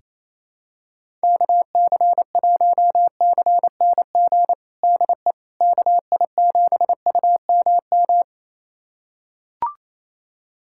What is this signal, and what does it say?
Morse code, heard as a keyed tone, about 28 wpm, 700 Hz. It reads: KC1CNG DE KI7UMM
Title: USA Callsign DE Callsign 28wpm